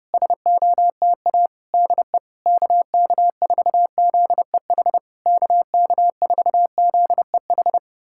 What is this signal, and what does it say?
Morse code, heard as a keyed tone, about 30 wpm, 700 Hz. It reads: SOTA DE KK4ZEH KK4ZEH
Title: SOTA QSO Practice 30wpm